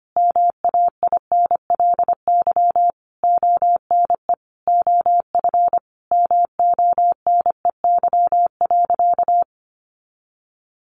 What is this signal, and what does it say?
Morse code, heard as a keyed tone, about 25 wpm, 700 Hz. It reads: MAINLY ONE OF MONEY.
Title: Sentences from Top 1400 Words 25wpm